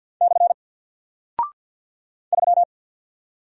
Morse code, heard as a keyed tone, about 48 wpm, 700 Hz. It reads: / 3
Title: Single Character (Letter:Number:Punctuation:Prosign) Alt 50wpm